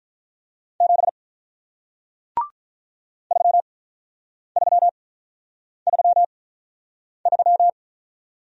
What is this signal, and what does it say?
Morse code, heard as a keyed tone, about 48 wpm, spaced as for 9 wpm, 700 Hz. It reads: 7 3333